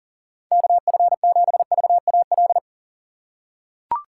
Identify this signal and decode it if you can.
Morse code, heard as a keyed tone, about 40 wpm, 700 Hz. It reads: KF7VAL
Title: USA Call Signs - 6x Character Spacing - 40wpm